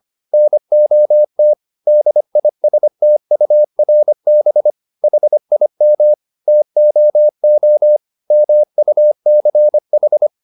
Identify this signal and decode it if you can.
Morse code, heard as a keyed tone, about 25 wpm, 600 Hz. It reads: NOT DISTURB HIM TOO MUCH